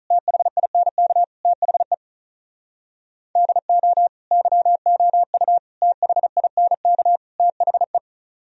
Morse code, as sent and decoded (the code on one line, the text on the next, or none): - .... .. -. -.- / - .... / -.. --- / -.-- --- ..- / - .... .. -. -.- / - .... .
THINK THE DO YOU THINK THE